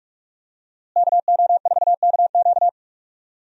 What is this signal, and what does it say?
Morse code, heard as a keyed tone, about 46 wpm, 700 Hz. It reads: KO4KQ
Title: USA Call Signs - 5 Characters - Repeated 3x - 45wpm